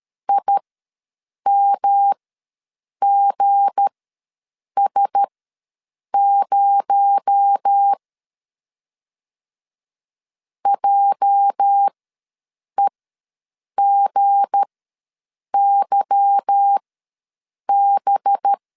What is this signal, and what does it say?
Morse code, heard as a keyed tone, about 13 wpm, spaced as for 7 wpm, 800 Hz. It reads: IMGS0 JEGYB
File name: LEZIONI / 8 YJ45 / 8 YJ ws 12wpm 3342